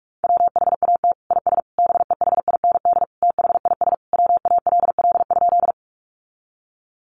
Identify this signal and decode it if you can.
Morse code, heard as a keyed tone, about 44 wpm, 700 Hz. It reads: WHAT IS BEHIND THIS WALL?
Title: Sentences from Top 1100 Words 45wpm